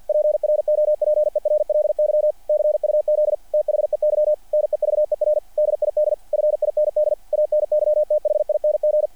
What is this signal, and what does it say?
Morse code, heard as a keyed tone, about 48 wpm, 600 Hz. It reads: PROPERLY CUZ THEY NEVER DID FIND ANYTHING
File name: archived / CRSnet